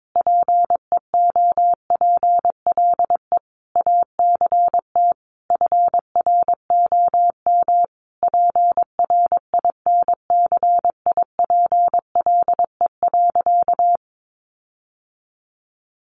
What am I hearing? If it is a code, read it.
Morse code, heard as a keyed tone, about 22 wpm, 700 Hz. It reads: PEOPLE ACT FROM PRINCIPLE.
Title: Sentences from Top 1400 Words 22wpm